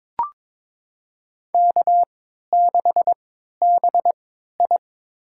Morse code, heard as a keyed tone, about 22 wpm, spaced as for 12 wpm, 700 Hz. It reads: K6BI
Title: USA Call Signs - 3x Character Spacing - 22wpm